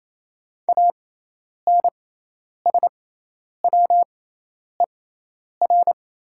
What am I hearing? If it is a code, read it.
Morse code, heard as a keyed tone, about 28 wpm, spaced as for 10 wpm, 700 Hz. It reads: ANSWER
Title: 6x Character Spacing - Top 500 Words - 28wpm